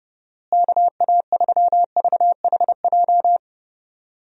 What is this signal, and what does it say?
Morse code, heard as a keyed tone, about 30 wpm, 700 Hz. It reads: KA3VHJ
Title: USA Call Signs - 6 Characters - Repeated 2x - 30wpm